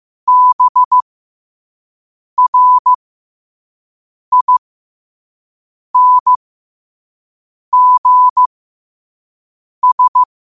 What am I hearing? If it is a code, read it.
Morse code, heard as a keyed tone, about 15 wpm, spaced as for 5 wpm, 1000 Hz. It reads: BRINGS